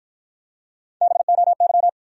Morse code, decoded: BOY

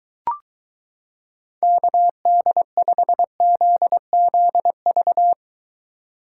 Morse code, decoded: KD5ZZV

